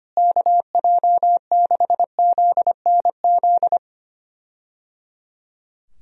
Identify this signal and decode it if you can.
Morse code, heard as a keyed tone, about 25 wpm, 700 Hz. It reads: KJ6ZNZ
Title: USA Call Signs - 6 Characters - Repeated 3x - 25wpm